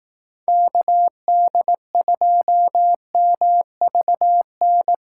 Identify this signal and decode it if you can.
Morse code, heard as a keyed tone, about 18 wpm, 700 Hz. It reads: KD2MVN
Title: Speed Racing - USA 2x3 Call Signs 15wpm